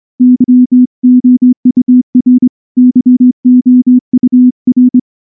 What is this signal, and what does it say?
Morse code, heard as a keyed tone, about 23 wpm, 250 Hz. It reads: YOUR YOUR
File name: morse / morse-training-25WPM